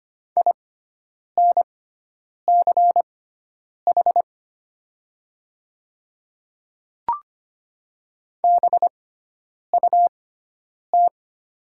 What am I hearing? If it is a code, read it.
Morse code, heard as a keyed tone, about 25 wpm, spaced as for 9 wpm, 700 Hz. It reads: INCH BUT
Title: 6x Character Spacing - Top 500 Words - 25wpm